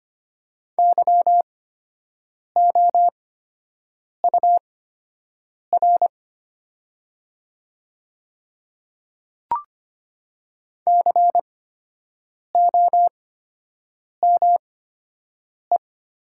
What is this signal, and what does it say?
Morse code, heard as a keyed tone, about 25 wpm, spaced as for 7 wpm, 700 Hz. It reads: YOUR COME
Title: 8x Character Spacing - Top 100 Words - 25wpm